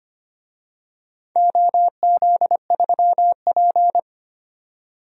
Morse code, heard as a keyed tone, about 25 wpm, 700 Hz. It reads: OZ3P
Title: DXCC-100 Call Signs - 4 Characters - 25wpm